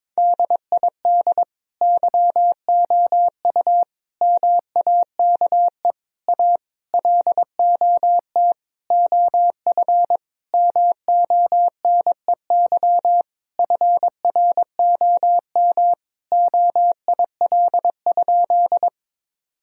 Morse code, heard as a keyed tone, about 22 wpm, 700 Hz. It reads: DID YOU MAKE A LOT OF MONEY FROM OIL?
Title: Sentences from Top 1000 Words - Review Part 4 - 22wpm